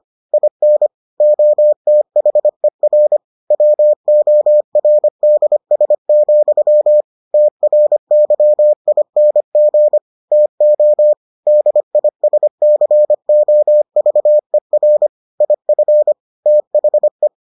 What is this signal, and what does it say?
Morse code, heard as a keyed tone, about 25 wpm, 600 Hz. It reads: IN OTHER WORDS, TRYING TO DISCOVER IF THE